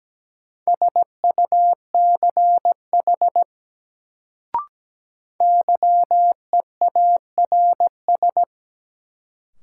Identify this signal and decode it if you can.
Morse code, heard as a keyed tone, about 17 wpm, 700 Hz. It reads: SUCH YEARS
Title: Sentences from Top 200 Words - New Word Drill - 17wpm